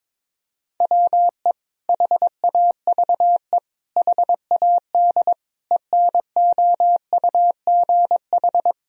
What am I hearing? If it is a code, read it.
Morse code, heard as a keyed tone, about 22 wpm, 700 Hz. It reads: WE HAVE HAD ENOUGH
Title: Sentences from Top 1000 Words - Review Part 3 - 22wpm